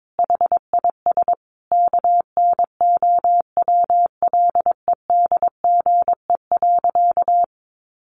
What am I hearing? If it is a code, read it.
Morse code, heard as a keyed tone, about 22 wpm, 700 Hz. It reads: HIS KNOWLEDGE.
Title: Sentences from Top 1300 Words 22wpm